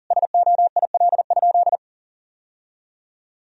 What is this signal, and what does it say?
Morse code, heard as a keyed tone, about 40 wpm, 700 Hz. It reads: SOIL?
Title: Sentences from Top 1000 Words - Review Part 4 - 40wpm